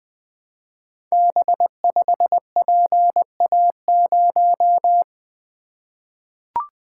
Morse code, heard as a keyed tone, about 20 wpm, 700 Hz. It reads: B5PA0
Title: Code Groups - 5 Letter:Numbers - 20wpm